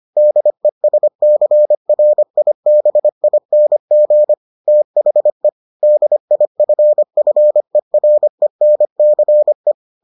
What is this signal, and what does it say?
Morse code, heard as a keyed tone, about 25 wpm, 600 Hz. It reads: DESCRIBING THE DIFFERENCE